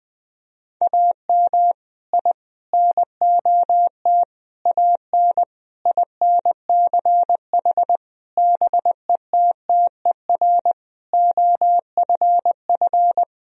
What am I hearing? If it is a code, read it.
Morse code, heard as a keyed tone, about 20 wpm, 700 Hz. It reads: AM I NOT AN INCH BETTER OFF
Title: Sentences from Top 1000 Words - Review Part 3 - 20wpm